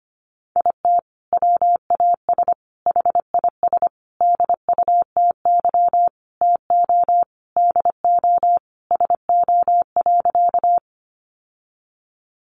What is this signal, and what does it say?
Morse code, heard as a keyed tone, about 25 wpm, 700 Hz. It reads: IT WAS HIS DUTY TO DO SO.
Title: Sentences from Top 1500 Words 25wpm